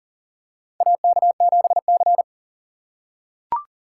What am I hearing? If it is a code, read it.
Morse code, heard as a keyed tone, about 40 wpm, 700 Hz. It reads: AK7C
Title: USA Call Signs - 4 Characters - 2x Character Spacing - 40wpm